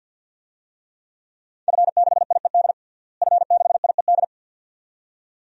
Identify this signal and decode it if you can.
Morse code, heard as a keyed tone, about 51 wpm, 700 Hz. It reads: F6IED F6IED
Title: DXCC-100 Call Signs - 5 Characters - Repeated 2x - 50wpm